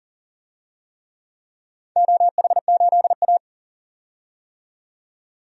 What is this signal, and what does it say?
Morse code, heard as a keyed tone, about 40 wpm, 700 Hz. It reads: OH8A
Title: DXCC-100 Call Signs - 4 Characters - 40wpm